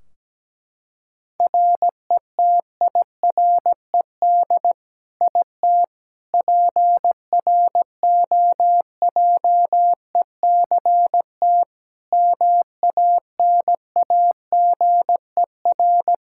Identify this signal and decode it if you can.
Morse code, heard as a keyed tone, about 17 wpm, 700 Hz. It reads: RETIRED IT PROJECT MANAGER
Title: Practice QSO - Exchange 3 - More Info - 17wpm